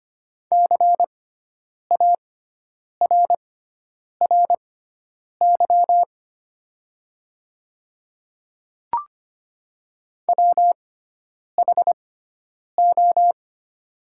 Morse code, heard as a keyed tone, about 25 wpm, spaced as for 9 wpm, 700 Hz. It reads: CARRY WHO
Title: 6x Character Spacing - Top 300 Words - 25wpm